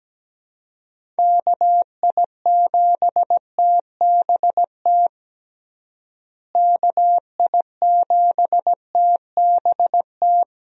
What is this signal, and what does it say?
Morse code, heard as a keyed tone, about 17 wpm, 700 Hz. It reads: KI7TBT KI7TBT
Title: USA Call Signs - 6 Characters - Repeated 3x - 17wpm